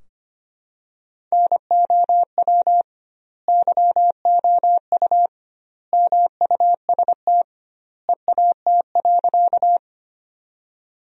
Morse code, heard as a keyed tone, about 25 wpm, 700 Hz. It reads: NOW YOU MUST EAT.
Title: 2x Word-Spacing - Sentences from Top 300 Words - 25wpm